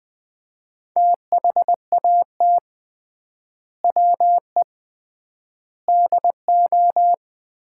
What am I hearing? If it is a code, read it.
Morse code, heard as a keyed tone, about 20 wpm, 700 Hz. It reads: THAT WE DO
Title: 3x Word-Spacing - Sets of 4 Words - 20wpm